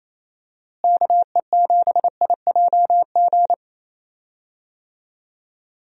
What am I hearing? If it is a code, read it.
Morse code, heard as a keyed tone, about 28 wpm, 700 Hz. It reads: KE7IJG
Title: USA Call Signs - 6 Characters - Repeated 2x - 28wpm